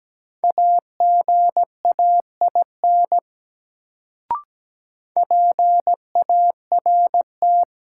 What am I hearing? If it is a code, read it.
Morse code, heard as a keyed tone, about 17 wpm, 700 Hz. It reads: AGAIN PART
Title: Sentences from Top 200 Words - New Word Drill - 17wpm